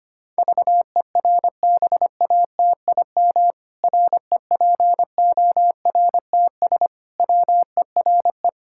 Morse code, heard as a keyed tone, about 25 wpm, 700 Hz. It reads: VERBATIM REPORTS WERE
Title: Pangram Challenge v2 - 25wpm